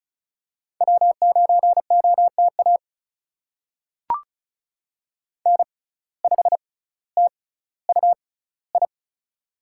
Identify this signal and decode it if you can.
Morse code, heard as a keyed tone, about 36 wpm, spaced as for 18 wpm, 700 Hz. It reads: W9OTA N5TUI
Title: USA Call Signs - 5 Characters - 6x Character Spacing - 35wpm